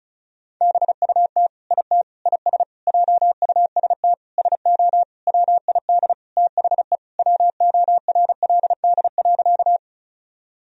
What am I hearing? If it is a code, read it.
Morse code, heard as a keyed tone, about 35 wpm, 700 Hz. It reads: BUT IT IS JUST SO WID THE WORLD.
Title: Sentences from Top 200 Words and Abbreviations 35wpm